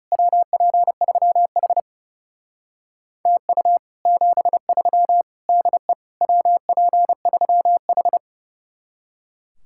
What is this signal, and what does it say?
Morse code, heard as a keyed tone, about 37 wpm, 700 Hz. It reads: WP3H TU 73 DE WP3H
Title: Speed-Racing Common QSO Phrases - 30wpm